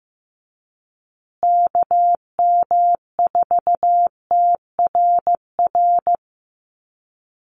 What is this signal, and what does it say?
Morse code, heard as a keyed tone, about 15 wpm, 700 Hz. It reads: KM4TRR
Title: USA Call Signs Repeated 2x 15wpm